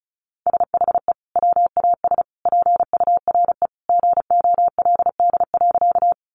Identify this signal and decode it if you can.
Morse code, heard as a keyed tone, about 34 wpm, 700 Hz. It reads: SHE WAS PURE GOLD.
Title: Sentences from Top 1500 Words 35wpm